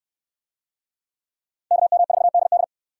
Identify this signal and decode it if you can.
Morse code, heard as a keyed tone, about 69 wpm, 700 Hz. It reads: KM4DD